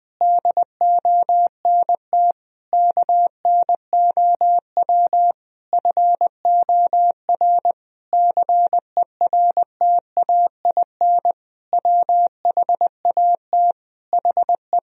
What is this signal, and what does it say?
Morse code, heard as a keyed tone, about 20 wpm, 700 Hz. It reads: DONT KNOW FOR CERTAIN WHAT HE
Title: Sentences from Top 1000 Words - Review Part 2 - 20wpm